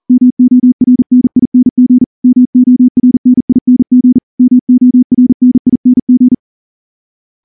Morse code, heard as a keyed tone, about 39 wpm, 250 Hz. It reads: MORNING MORNING MORNING
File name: morse-training-40WPM